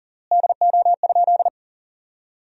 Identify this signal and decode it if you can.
Morse code, heard as a keyed tone, about 40 wpm, 700 Hz. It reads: DO?
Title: 3x Word-Spacing - Sentences from Top 400 Words - 40wpm